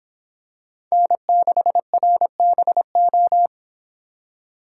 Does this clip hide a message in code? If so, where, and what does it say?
Yes: Morse code from 0.9 to 3.5 s, heard as a keyed tone, about 24 wpm, 700 Hz. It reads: N6RBO